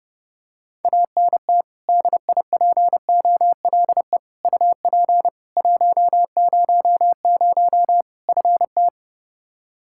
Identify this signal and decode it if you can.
Morse code, heard as a keyed tone, about 30 wpm, 700 Hz. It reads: ANT DIPOLE UP 100 FT
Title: Practice QSO - Exchange 2 - Info - 30wpm